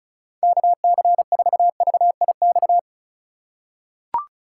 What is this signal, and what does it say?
Morse code, heard as a keyed tone, about 36 wpm, 690 Hz. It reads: KC4VIX
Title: USA Call Signs - 6 Characters - 4x Character Spacing - 35wpm